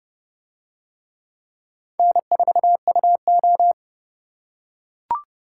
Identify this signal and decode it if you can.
Morse code, heard as a keyed tone, about 28 wpm, 700 Hz. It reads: N4UO